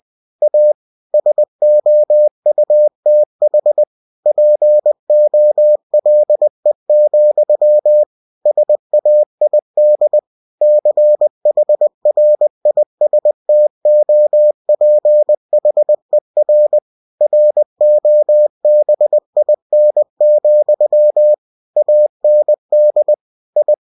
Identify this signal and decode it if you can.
Morse code, heard as a keyed tone, about 20 wpm, 600 Hz. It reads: A SOUTH POLE, SAID CHRISTOPHER ROBIN, AND I
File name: cw-books / winnie-the-pooh / winnie-20wpm-page-0093